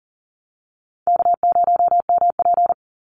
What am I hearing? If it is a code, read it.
Morse code, heard as a keyed tone, about 40 wpm, 700 Hz. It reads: K0MP
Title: Call Signs Repeated 2x - 40wpm